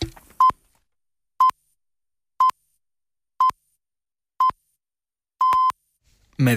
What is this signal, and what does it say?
Radio time signal (time pips), with speech. It is heard from 0.4 to 5.7 s.